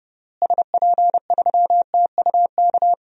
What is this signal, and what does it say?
Morse code, heard as a keyed tone, about 29 wpm, 700 Hz. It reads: SP3TUK
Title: DXCC-100 Call Signs - 6 Characters - Repeated 3x - 30wpm